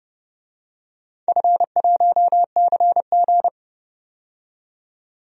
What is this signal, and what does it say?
Morse code, heard as a keyed tone, about 30 wpm, 700 Hz. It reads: F1CG
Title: DXCC-100 Call Signs - 4 Characters - 30wpm